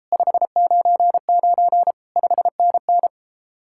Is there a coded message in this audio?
Yes: Morse code from 0.1 to 3.1 s, heard as a keyed tone, about 33 wpm, 700 Hz. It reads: 599 5NN